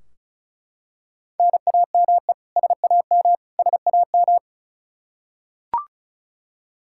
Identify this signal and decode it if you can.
Morse code, heard as a keyed tone, about 36 wpm, 700 Hz. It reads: NAME SAM SAM
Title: Practice QSO - Exchange 1 - Beginning - 35wpm